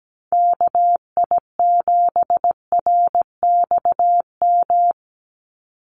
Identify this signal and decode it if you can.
Morse code, heard as a keyed tone, about 17 wpm, 700 Hz. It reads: KI7RXM
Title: USA Call Signs 17wpm